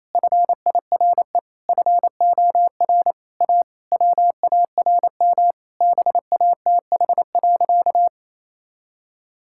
Morse code, heard as a keyed tone, about 28 wpm, 700 Hz. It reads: FIRE FOR A WARM BATH.
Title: Sentences from Top 1600 Words 28wpm